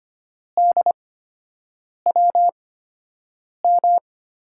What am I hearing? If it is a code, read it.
Morse code, heard as a keyed tone, about 25 wpm, spaced as for 7 wpm, 700 Hz. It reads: DWM
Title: USA Call Signs - 5 Characters - 8x Character Spacing - 25wpm